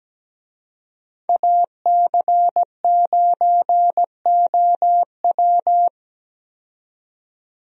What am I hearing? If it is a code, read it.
Morse code, heard as a keyed tone, about 17 wpm, 700 Hz. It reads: AC9OW
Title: USA Call Signs - 5 Characters - 17wpm